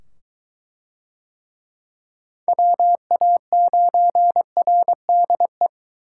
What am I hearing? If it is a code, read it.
Morse code, heard as a keyed tone, about 20 wpm, 700 Hz. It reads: WA9RDE